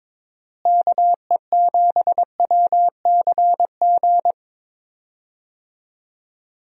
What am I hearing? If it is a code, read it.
Morse code, heard as a keyed tone, about 21 wpm, 700 Hz. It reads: KE7WCG
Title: USA Call Signs - 6 Characters - Repeated 3x - 22wpm